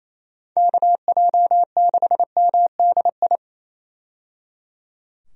Morse code, heard as a keyed tone, about 28 wpm, 700 Hz. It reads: KJ6MDI
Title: USA Call Signs - 6 Characters - 28wpm